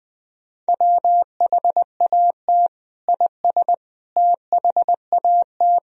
Morse code, heard as a keyed tone, about 21 wpm, 700 Hz. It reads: WHAT IS THAT
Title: Sentences from Top 1000 Words - Review Part 2 - 20wpm